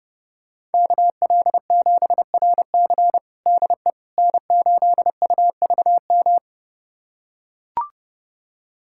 Morse code, heard as a keyed tone, about 30 wpm, 700 Hz. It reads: KL7RC DE N8UVM
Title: USA Callsign DE Callsign - 5 Characters - 30wpm